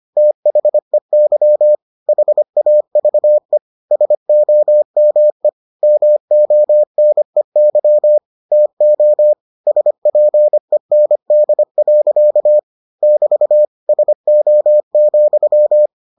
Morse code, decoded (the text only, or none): THEY HAVE SOME MONEY TO SPEND. <BT> SO,